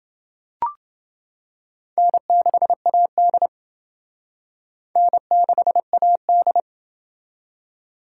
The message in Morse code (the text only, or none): N6AD N6AD